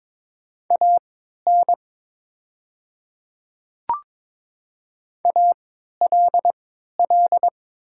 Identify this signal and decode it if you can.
Morse code, heard as a keyed tone, about 22 wpm, spaced as for 12 wpm, 700 Hz. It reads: AN ALL
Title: 3x Character Spacing - Top 100 Words - 22wpm